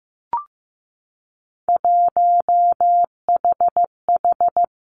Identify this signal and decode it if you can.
Morse code, heard as a keyed tone, about 15 wpm, 700 Hz. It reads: 1HH
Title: Three Challenging Letter-Number Combinations 15wpm